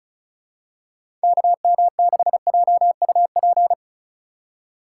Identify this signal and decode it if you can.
Morse code, heard as a keyed tone, about 34 wpm, 700 Hz. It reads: KM6JUP
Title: USA Call Signs - 6 Characters - 4x Character Spacing - 35wpm